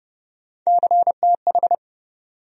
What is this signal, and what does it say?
Morse code, heard as a keyed tone, about 30 wpm, 700 Hz. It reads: CTH
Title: USA Call Sign Postfixes - 3 Letters 30wpm